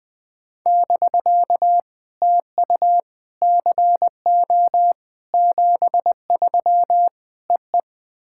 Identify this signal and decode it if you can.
Morse code, heard as a keyed tone, about 20 wpm, 700 Hz. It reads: <BK> TU CO 73 EE
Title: POTA QSO Practice - 20wpm